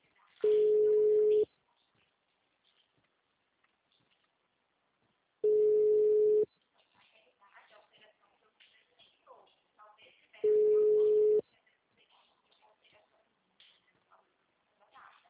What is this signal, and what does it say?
Ringback tone (Europe, 425 Hz), with speech.